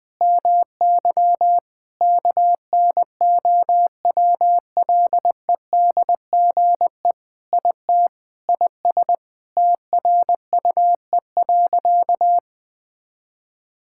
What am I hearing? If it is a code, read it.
Morse code, heard as a keyed tone, about 20 wpm, 700 Hz. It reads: MY KNOWLEDGE IT IS TRUE.
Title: Sentences with 6-Word Phrases - 20wpm